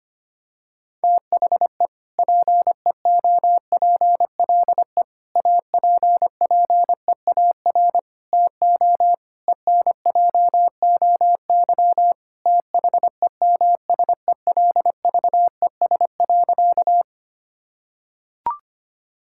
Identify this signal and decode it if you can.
Morse code, heard as a keyed tone, about 25 wpm, 700 Hz. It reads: THE PEOPLE APPEAR TO ENJOY THEMSELVES.
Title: Sentences from Top 1000 Words - Review Part 5 - 25wpm